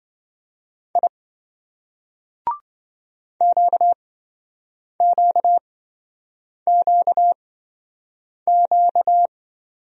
Morse code, heard as a keyed tone, about 26 wpm, spaced as for 7 wpm, 700 Hz. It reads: I QQQQ